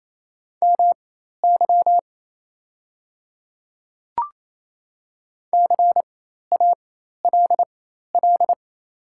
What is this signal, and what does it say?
Morse code, heard as a keyed tone, about 28 wpm, spaced as for 13 wpm, 700 Hz. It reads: MY CALL
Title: 4x Character Spacing - Top 100 Words - 28wpm